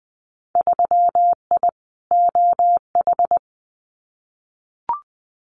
Morse code, heard as a keyed tone, about 20 wpm, 700 Hz. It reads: 3I OH
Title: Winter Field Day Exchanges Encore 20wpm